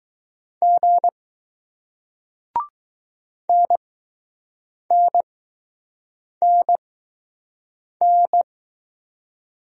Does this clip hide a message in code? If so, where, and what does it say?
Yes: Morse code from 0.6 to 8.4 s, heard as a keyed tone, about 20 wpm, spaced as for 6 wpm, 700 Hz. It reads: G NNNN